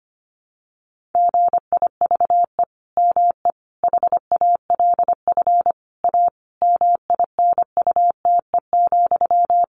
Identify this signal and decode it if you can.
Morse code, heard as a keyed tone, about 25 wpm, 700 Hz. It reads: GIVE ME HALF A MINUTE,
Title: Sentences from Top 1500 Words 25wpm